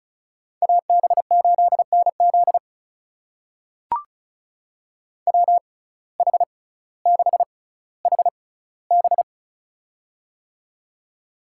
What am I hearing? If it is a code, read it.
Morse code, heard as a keyed tone, about 36 wpm, spaced as for 18 wpm, 700 Hz. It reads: AB8NZ WH6HB
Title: USA Call Signs - 5 Characters - 6x Character Spacing - 35wpm